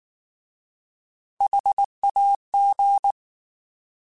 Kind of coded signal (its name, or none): Morse code